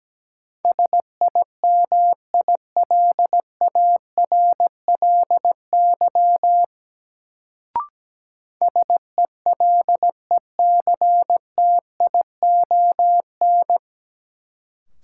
Morse code, heard as a keyed tone, about 17 wpm, 700 Hz. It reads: SIMILARLY SELECTION
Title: Sentences from Top 1400 Words - New Word Drill - 17wpm